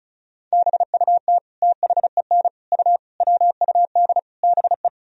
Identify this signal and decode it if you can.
Morse code, heard as a keyed tone, about 36 wpm, 700 Hz. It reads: BUT THEN U WUD BE
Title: Sentences from Top 300 Words and Abbreviations 35wpm